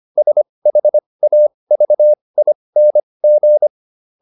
Morse code, heard as a keyed tone, about 25 wpm, spaced as for 20 wpm, 600 Hz. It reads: SHAVING